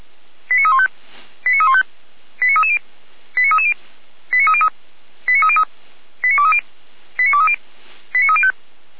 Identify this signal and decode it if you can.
Five-tone selective call (ZVEI), heard as quick runs of five tones, tones of about 70 ms. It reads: 89315 89315 89330 89330 89303 89303 89230 89230 89305